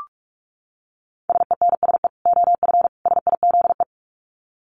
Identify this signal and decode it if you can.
Morse code, heard as a keyed tone, about 44 wpm, 700 Hz. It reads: SENSE OF SIZE
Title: Dit Challenge 45wpm